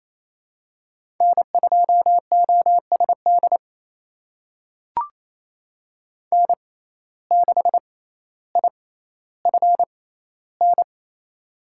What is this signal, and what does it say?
Morse code, heard as a keyed tone, about 28 wpm, spaced as for 14 wpm, 700 Hz. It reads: N2OSD N6IFN
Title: USA Call Signs - 6x Character Spacing - 28wpm